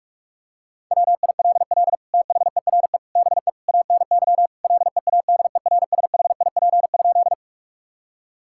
Morse code, heard as a keyed tone, about 41 wpm, 700 Hz. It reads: WILL THERE BE ANY LEADERSHIP?
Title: Sentences from Top 1600 Words 45wpm